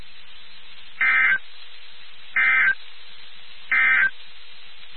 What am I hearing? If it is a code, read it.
Emergency Alert System (SAME) tones, heard as harsh buzzing bursts. It is heard from 0.9 to 4.2 s.
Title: NOAA Weather Radio WXK79